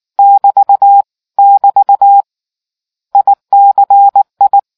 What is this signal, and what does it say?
Morse code, heard as a keyed tone, about 19 wpm, 800 Hz. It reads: <BT> <BT> ICI